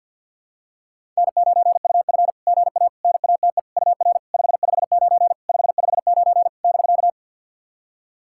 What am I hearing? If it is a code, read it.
Morse code, heard as a keyed tone, about 49 wpm, 660 Hz. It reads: N9UF GA NATE UR 559 559 <BK>
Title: SOTA QSO Practice 50wpm